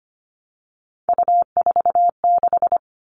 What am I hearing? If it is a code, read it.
Morse code, heard as a keyed tone, about 25 wpm, 700 Hz. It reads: U46
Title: All Cut Numbers - 3-Digit 25wpm